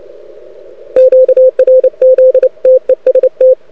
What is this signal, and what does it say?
Morse code, heard as a keyed tone, about 29 wpm, 500 Hz. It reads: QRZ TEST
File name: rec contest / arrl cw 2018 / test 32